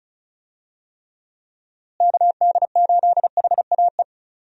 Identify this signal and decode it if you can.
Morse code, heard as a keyed tone, about 36 wpm, 700 Hz. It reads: KD8HAE
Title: USA Call Signs - 6 Characters - Repeated 3x - 35wpm